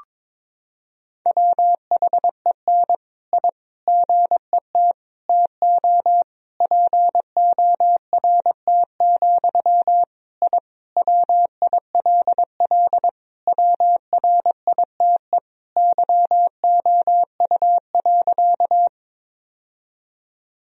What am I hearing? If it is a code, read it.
Morse code, heard as a keyed tone, about 22 wpm, 700 Hz. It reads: WHEN I GET TO PORT, I WILL WRITE YOU.
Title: Sentences from Top 1000 Words - Review Part 1 - 22wpm